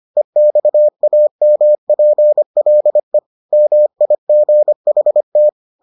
Morse code, heard as a keyed tone, about 25 wpm, 600 Hz. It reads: EXAMPLE MIGHT